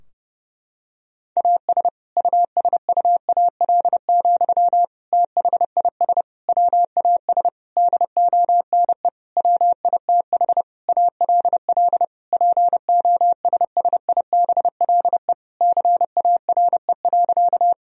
Morse code, heard as a keyed tone, about 31 wpm, 700 Hz. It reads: AS USUAL, THIS WAS DONE WITH ALL POSSIBLE CARE.
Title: Sentences from Top 1000 Words - Review Part 2 - 30wpm